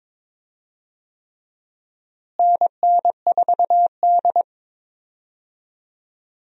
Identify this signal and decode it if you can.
Morse code, heard as a keyed tone, about 22 wpm, 700 Hz. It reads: NN4D
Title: USA Call Signs - 4 Characters - Repeated 3x - 22wpm